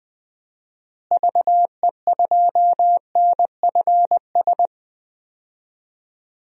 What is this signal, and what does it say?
Morse code, heard as a keyed tone, about 20 wpm, 700 Hz. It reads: VE2NFS
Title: DXCC-100 Call Signs - 6 Characters - Repeated 2x - 20wpm